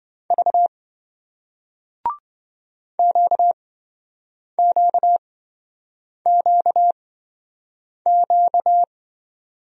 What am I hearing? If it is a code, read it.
Morse code, heard as a keyed tone, about 27 wpm, spaced as for 7 wpm, 700 Hz. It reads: V QQQQ